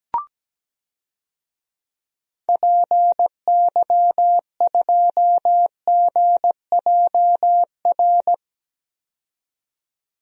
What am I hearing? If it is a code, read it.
Morse code, heard as a keyed tone, about 17 wpm, 700 Hz. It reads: PY2GJR
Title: DXCC-100 Call Signs - 6 Characters - 17wpm